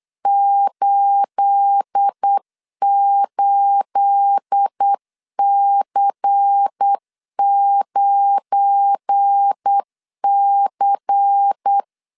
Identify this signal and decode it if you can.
Morse code, heard as a keyed tone, about 8 wpm, 800 Hz. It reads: 88C9C
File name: CW / DET / 1 9 1500C S B 8wpm 5624